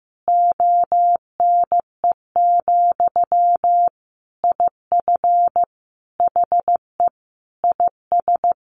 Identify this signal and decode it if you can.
Morse code, heard as a keyed tone, about 15 wpm, 700 Hz. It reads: ONE, IF HE IS
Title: Sentences from Top 900 Words 15wpm